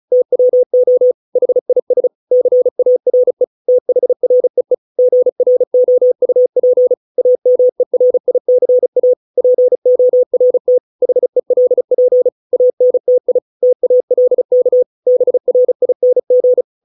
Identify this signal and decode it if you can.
Morse code, heard as a keyed tone, about 34 wpm, 500 Hz. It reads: TWO HIS CARE THREE GROUP AMERICA PORT HELP ANTI TALK BRING